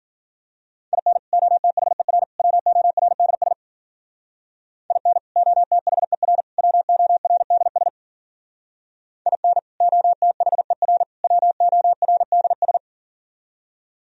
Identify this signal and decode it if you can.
Morse code, heard as a keyed tone, about 46 wpm, 700 Hz. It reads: IN OTHER WORDS IN OTHER WORDS IN OTHER WORDS
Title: Speed-Racing - Sets of 3 Words 40wpm